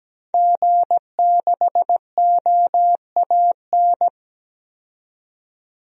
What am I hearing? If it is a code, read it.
Morse code, heard as a keyed tone, about 17 wpm, 700 Hz. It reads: G6OAN
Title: DXCC-100 Call Signs - 5 Characters - 17wpm